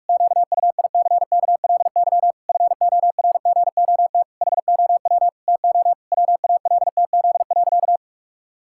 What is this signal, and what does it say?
Morse code, heard as a keyed tone, about 41 wpm, 700 Hz. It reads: QUICKLY FORGOT HOW TO WALTZ.
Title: Pangram Challenge v2 - 45wpm